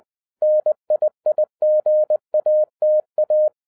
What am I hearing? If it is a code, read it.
Morse code, heard as a keyed tone, about 20 wpm, 610 Hz. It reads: NIIGATA
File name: 2011-10-mp3 / n03-TARO JR0QWW 20wpm